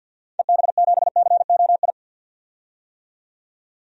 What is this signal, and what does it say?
Morse code, heard as a keyed tone, about 52 wpm, 700 Hz. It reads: EB7COI
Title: DXCC-100 Call Signs - 6 Characters - 50wpm